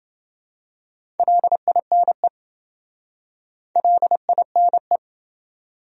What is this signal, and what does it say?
Morse code, heard as a keyed tone, about 28 wpm, 700 Hz. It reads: LINE LINE